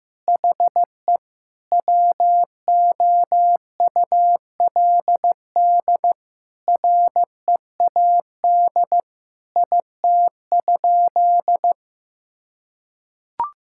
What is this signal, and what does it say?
Morse code, heard as a keyed tone, about 15 wpm, 700 Hz. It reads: HE WOULD READ IT?
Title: Sentences from Top 1600 Words 15wpm